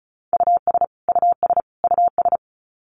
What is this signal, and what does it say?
Morse code, heard as a keyed tone, about 34 wpm, 700 Hz. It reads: US US US
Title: Top 200 Words Repeated 35wpm